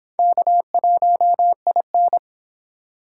Morse code, heard as a keyed tone, about 31 wpm, 700 Hz. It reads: K1IN